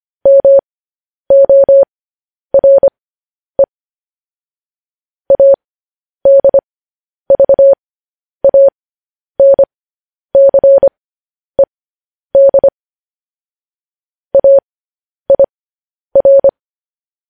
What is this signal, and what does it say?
Morse code, heard as a keyed tone, about 25 wpm, spaced as for 10 wpm, 550 Hz. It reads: MORE ADVANCED AIR
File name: podcast / News headlines-1x-25x10